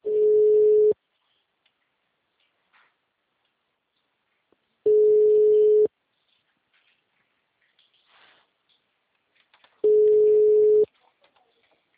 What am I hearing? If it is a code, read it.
Ringback tone (Europe, 425 Hz).
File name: out- -2026- - - .20948